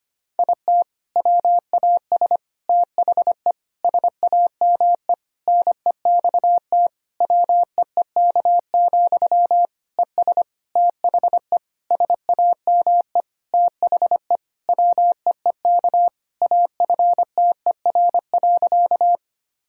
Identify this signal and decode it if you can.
Morse code, heard as a keyed tone, about 25 wpm, 700 Hz. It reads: IT WAS THE SAME NEXT WEEK, ES THE SAME THE WEEK AFTER.
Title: Sentences from Top 500 Words and Abbreviations 25wpm